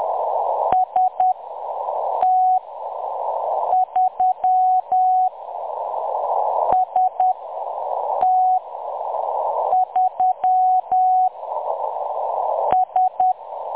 Morse code, heard as a keyed tone, about 10 wpm, spaced as for 6 wpm, 740 Hz. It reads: ST3 ST3 S
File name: M23-ST3- -0858utc-10184khz